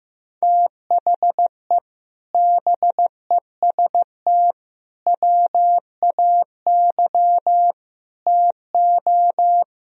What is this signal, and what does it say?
Morse code, heard as a keyed tone, about 15 wpm, 700 Hz. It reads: THE BEST WAY TO
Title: Speed-Racing - Sets of 4 Words 15wpm